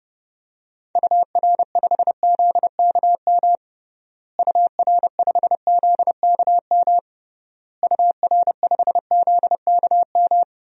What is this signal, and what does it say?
Morse code, heard as a keyed tone, about 30 wpm, 700 Hz. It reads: UR5ZKM UR5ZKM UR5ZKM
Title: DXCC-100 Call Signs - 6 Characters - Repeated 3x - 30wpm